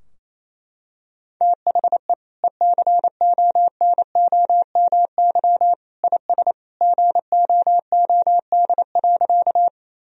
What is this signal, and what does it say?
Morse code, heard as a keyed tone, about 27 wpm, 690 Hz. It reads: THE ECONOMY IS GOOD.
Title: Sentences from Top 1000 Words - Review Part 3 - 28wpm